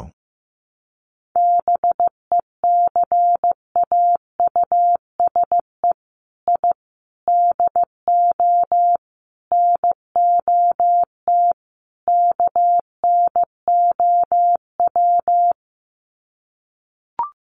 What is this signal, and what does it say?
Morse code, heard as a keyed tone, about 15 wpm, 700 Hz. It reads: BECAUSE I DO NOT KNOW
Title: Sets of 5 Words Encore2 15wpm